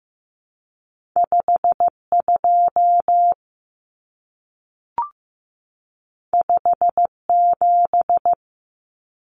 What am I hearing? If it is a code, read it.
Morse code, heard as a keyed tone, about 15 wpm, 700 Hz. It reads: 52 57
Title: Common Cut Numbers - 2-Digit 15wpm